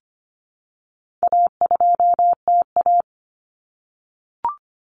Morse code, heard as a keyed tone, about 25 wpm, 700 Hz. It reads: A2TA
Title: Common Cut Numbers - 4-Digit 25wpm